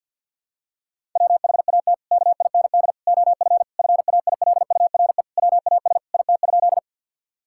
Morse code, heard as a keyed tone, about 50 wpm, 700 Hz. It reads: WHAT KIND OF FAILURE WAS IT?
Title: Sentences from Top 1000 Words - Review Part 6 - 50wpm